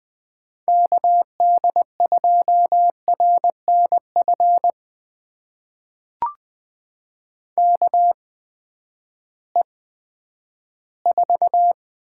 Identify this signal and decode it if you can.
Morse code, heard as a keyed tone, about 20 wpm, spaced as for 11 wpm, 700 Hz. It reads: KD2RNF KE4
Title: USA Call Signs - 6 Characters - 8x Character Spacing - 20wpm